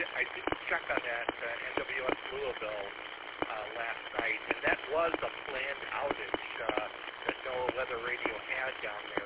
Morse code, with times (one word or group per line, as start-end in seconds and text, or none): none